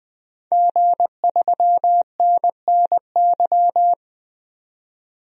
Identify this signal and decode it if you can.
Morse code, heard as a keyed tone, about 20 wpm, 700 Hz. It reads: G3NNY
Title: DXCC-100 Call Signs - 5 Characters - 20wpm